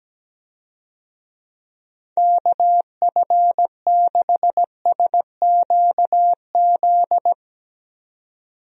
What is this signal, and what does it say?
Morse code, heard as a keyed tone, about 17 wpm, 700 Hz. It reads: KF6SQZ
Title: USA Call Signs - 6 Characters - 17wpm